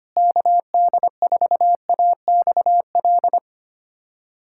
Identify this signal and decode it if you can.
Morse code, heard as a keyed tone, about 25 wpm, 700 Hz. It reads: KD4AXL
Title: USA Callsign DE Callsign - 6 Characters - 25wpm